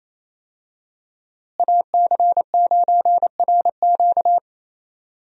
Morse code, heard as a keyed tone, about 27 wpm, 700 Hz. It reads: AC9RQ
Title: USA Call Signs - 5 Characters - Repeated 2x - 28wpm